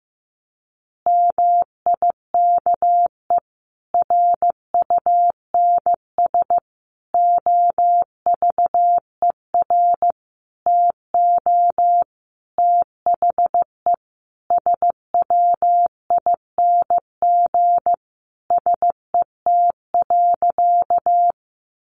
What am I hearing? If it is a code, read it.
Morse code, heard as a keyed tone, about 15 wpm, 700 Hz. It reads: MIKE RUNS OVER TO THE SWING SET.
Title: Additional Easy Phrases 15wpm